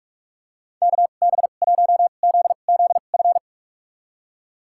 Morse code, heard as a keyed tone, about 46 wpm, spaced as for 33 wpm, 700 Hz. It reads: KB1ZZF
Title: USA Call Signs - 2x Character Spacing - 45wpm